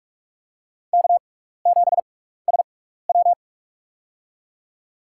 Morse code, heard as a keyed tone, about 46 wpm, spaced as for 15 wpm, 700 Hz. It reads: K7SW
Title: USA Call Signs - 6x Character Spacing - 45wpm